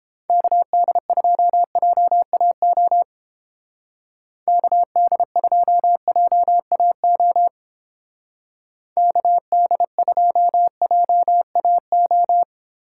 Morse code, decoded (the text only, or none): KD2JAO KD2JAO KD2JAO